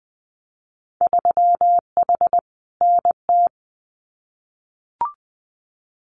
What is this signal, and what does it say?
Morse code, heard as a keyed tone, about 20 wpm, 700 Hz. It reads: 3H NT
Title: Winter Field Day Exchanges Encore 20wpm